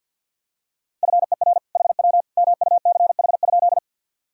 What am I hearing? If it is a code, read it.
Morse code, heard as a keyed tone, about 50 wpm, 700 Hz. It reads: FER HW MUCH?
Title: Sentences from Top 200 Words and Abbreviations 50wpm